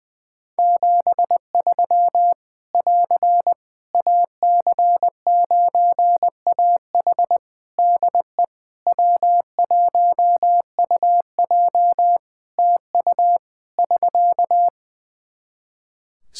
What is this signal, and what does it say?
Morse code, heard as a keyed tone, about 20 wpm, 700 Hz. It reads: 73 <AR> AC9AH DE W1UJ TU <SK>
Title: K1USN SST QSO Practice 20wpm